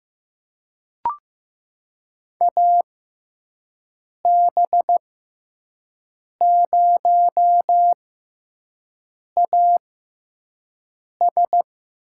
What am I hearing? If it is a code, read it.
Morse code, heard as a keyed tone, about 15 wpm, spaced as for 5 wpm, 700 Hz. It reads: AB0AS
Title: USA Call Signs - 5 Characters - 6x Character Spacing - 15wpm